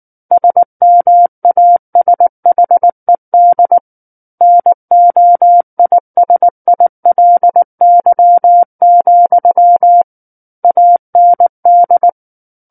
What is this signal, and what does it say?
Morse code, heard as a keyed tone, about 19 wpm, 700 Hz. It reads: SMASHED NOISILY, AND
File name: Book One / 10-Chapter 10 - 19 WPM